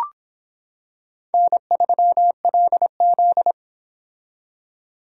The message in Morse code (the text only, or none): N3LZ